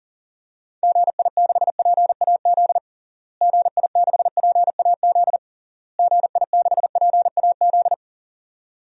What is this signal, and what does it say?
Morse code, heard as a keyed tone, about 37 wpm, 700 Hz. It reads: GI6PAZ GI6PAZ GI6PAZ
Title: DXCC-100 Call Signs - 6 Characters - Repeated 3x - 40wpm